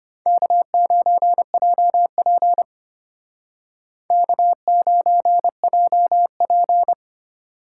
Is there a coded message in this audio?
Yes: Morse code from 0.3 to 6.9 s, heard as a keyed tone, about 33 wpm, 700 Hz. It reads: K9JP K9JP